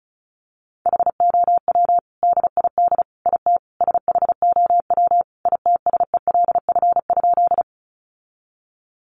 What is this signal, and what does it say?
Morse code, heard as a keyed tone, about 34 wpm, 700 Hz. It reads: HOW DID IT SHOW ITSELF?
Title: Sentences from Top 800 Words 35wpm